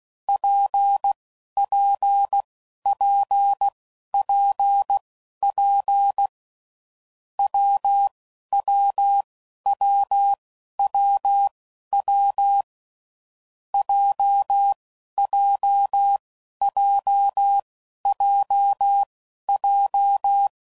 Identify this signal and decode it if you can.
Morse code, heard as a keyed tone, about 16 wpm, spaced as for 11 wpm, 800 Hz. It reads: PPPPP WWWWW JJJJJ
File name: Combinations / Difficult Combos / WPJ WPJ1